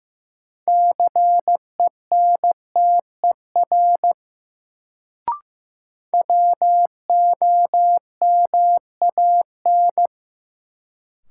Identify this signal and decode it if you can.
Morse code, heard as a keyed tone, about 15 wpm, 700 Hz. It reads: CENTER WOMAN
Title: Sentences from Top 1000 Words - Review Part 2 - New Word Drill - 15wpm